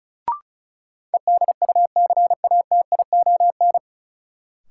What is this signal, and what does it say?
Morse code, heard as a keyed tone, about 34 wpm, 700 Hz. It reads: EDUCATION
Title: Sentences from Top 1000 Words - Review Part 6 - New Word Drill - 35wpm